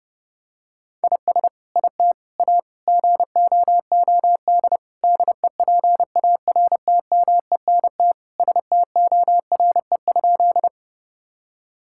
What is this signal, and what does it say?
Morse code, heard as a keyed tone, about 32 wpm, 700 Hz. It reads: IS IT A GOOD DEPARTMENT STORE?
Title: Sentences from Top 1000 Words - Review Part 5 - 30wpm